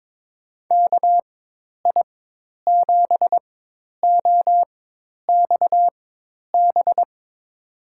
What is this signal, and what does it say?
Morse code, heard as a keyed tone, about 22 wpm, spaced as for 10 wpm, 700 Hz. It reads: KI7OXB
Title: USA Call Signs - 6 Characters - 4x Character Spacing - 22wpm